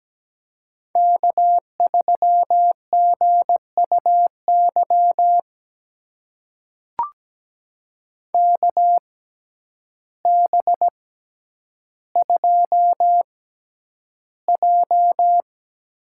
Morse code, heard as a keyed tone, about 17 wpm, spaced as for 9 wpm, 700 Hz. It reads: K3GUY KB2J
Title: USA Call Signs - 6x Character Spacing - 17wpm